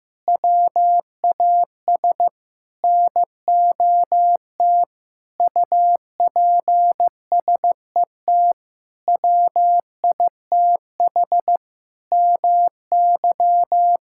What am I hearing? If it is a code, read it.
Morse code, heard as a keyed tone, about 15 wpm, 700 Hz. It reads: WAS NOT UPSET WITH MY
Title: Sentences from Top 1000 Words - Review Part 5 - 15wpm